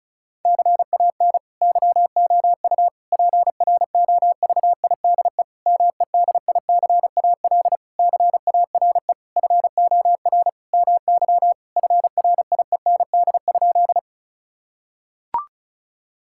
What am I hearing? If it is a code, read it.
Morse code, heard as a keyed tone, about 36 wpm, 700 Hz. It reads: CAN YOU PROVIDE MEDICAL CARE FOR MY FRIEND?
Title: Sentences from Top 1000 Words - Review Part 5 - 35wpm